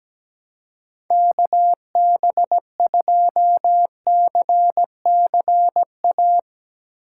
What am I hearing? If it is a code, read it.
Morse code, heard as a keyed tone, about 17 wpm, 700 Hz. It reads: KB2CCA
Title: USA Call Signs - 6 Characters - 3x Character Spacing - 17wpm